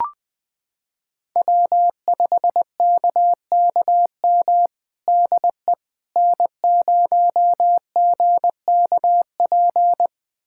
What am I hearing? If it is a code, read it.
Morse code, heard as a keyed tone, about 21 wpm, 700 Hz. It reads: W5KKM DE N0GKP
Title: USA Callsign DE Callsign - 5 Characters - 20wpm